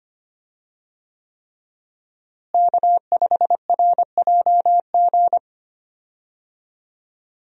Morse code, heard as a keyed tone, about 25 wpm, 700 Hz. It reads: K5RJG
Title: USA Call Signs - 5 Characters - Repeated 3x - 25wpm